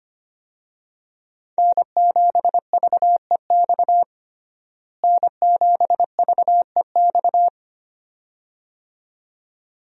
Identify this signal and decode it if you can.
Morse code, heard as a keyed tone, about 25 wpm, 700 Hz. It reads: N7VEX N7VEX
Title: USA Call Signs - 5 Characters - Repeated 2x - 25wpm